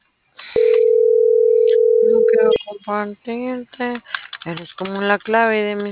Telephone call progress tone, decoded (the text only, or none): ringback tone